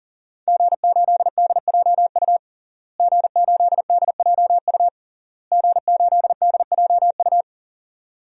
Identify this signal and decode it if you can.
Morse code, heard as a keyed tone, about 40 wpm, 700 Hz. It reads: G8DJU G8DJU G8DJU
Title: DXCC-100 Call Signs - 5 Characters - Repeated 3x - 40wpm